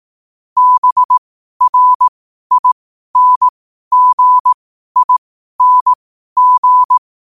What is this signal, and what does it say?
Morse code, heard as a keyed tone, about 18 wpm, spaced as for 12 wpm, 1000 Hz. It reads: BRINGING
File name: 12wpm